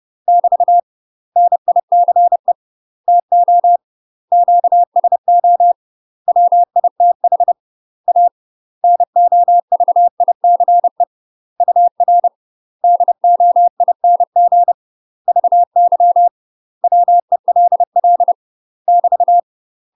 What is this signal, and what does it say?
Morse code, heard as a keyed tone, about 30 wpm, 700 Hz. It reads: <BT> NICE TO QSO WITH A NOVICE UR DOING VY WELL <BT>
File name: UK MASTER QSOs TRAINING / 30WPM / UK MASTER QSOs 0032